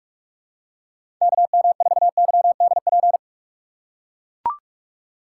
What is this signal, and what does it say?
Morse code, heard as a keyed tone, about 42 wpm, 700 Hz. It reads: KM4YDP
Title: USA Call Signs - 6 Characters - 4x Character Spacing - 45wpm